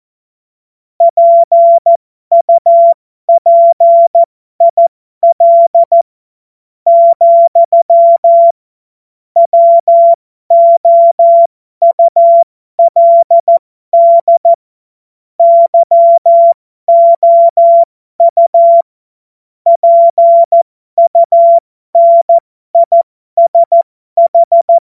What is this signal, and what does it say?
Morse code, heard as a keyed tone, about 14 wpm, spaced as for 12 wpm, 680 Hz. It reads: PUPIL, WOULD YOU PUNISH
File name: Morsekurs / Witze